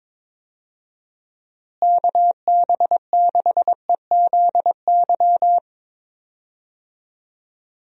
Morse code, heard as a keyed tone, about 22 wpm, 700 Hz. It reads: KB6EZY
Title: USA Call Signs - 6 Characters - Repeated 3x - 22wpm